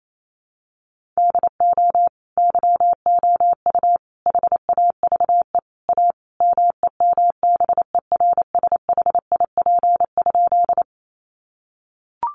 Morse code, heard as a keyed tone, about 28 wpm, 700 Hz. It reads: DO YOU HAVE A MEMBERSHIP?
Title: Sentences from Top 1400 Words 28wpm